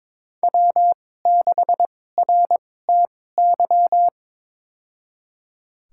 Morse code, heard as a keyed tone, about 22 wpm, spaced as for 16 wpm, 700 Hz. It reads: W6RTY
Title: USA Call Signs - 5 Characters - 2x Character Spacing - 22wpm